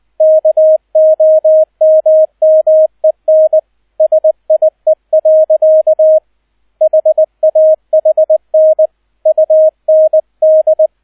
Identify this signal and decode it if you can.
Morse code, heard as a keyed tone, about 20 wpm, 630 Hz, with radio noise. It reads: KOMMEN SIE. HAHN UND
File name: Erster Streich 20wpm